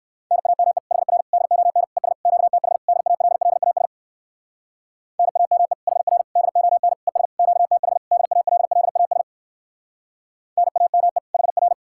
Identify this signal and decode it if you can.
Morse code, heard as a keyed tone, about 58 wpm, 700 Hz. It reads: NAME HR DON ES QTH DALLAS NAME HR DON ES QTH DALLAS NAME HR